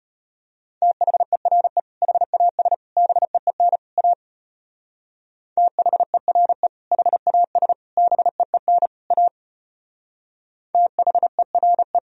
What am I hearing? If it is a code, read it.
Morse code, heard as a keyed tone, about 32 wpm, 700 Hz. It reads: THERE HAS BEEN A THERE HAS BEEN A THERE